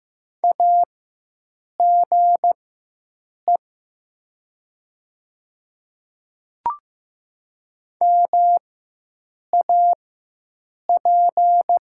Morse code, heard as a keyed tone, about 15 wpm, spaced as for 7 wpm, 700 Hz. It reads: AGE MAP
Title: 4x Character Spacing - Top 500 Words - 15wpm